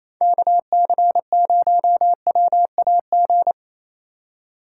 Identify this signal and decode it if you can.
Morse code, heard as a keyed tone, about 28 wpm, 700 Hz. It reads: KC0WAG
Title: USA Call Signs - 6x Character Spacing - 28wpm